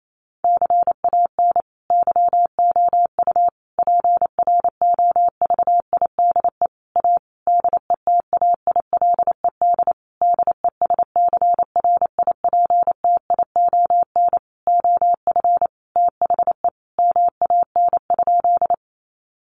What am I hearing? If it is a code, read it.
Morse code, heard as a keyed tone, about 28 wpm, 700 Hz. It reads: CAN YOU PROVIDE A DETAILED DESCRIPTION OF THE MAN?
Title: Sentences from Top 1500 Words 28wpm